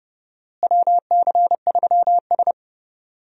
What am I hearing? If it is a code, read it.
Morse code, heard as a keyed tone, about 30 wpm, 700 Hz. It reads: WC3S
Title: USA Call Signs - 4 Characters - Repeated 3x - 30wpm